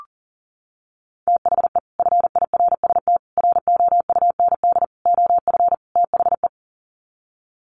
Morse code, heard as a keyed tone, about 40 wpm, 700 Hz. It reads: THE FIRST ROUND OF THE
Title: Sets of 5 Words Encore 40wpm